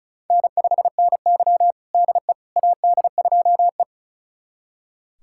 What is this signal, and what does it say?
Morse code, heard as a keyed tone, about 36 wpm, 700 Hz. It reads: N5NY DE AD2E
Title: USA Callsign DE Callsign - 4 Characters - 35wpm